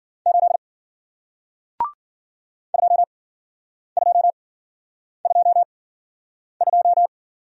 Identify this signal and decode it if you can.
Morse code, heard as a keyed tone, about 50 wpm, spaced as for 9 wpm, 700 Hz. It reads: 8 2222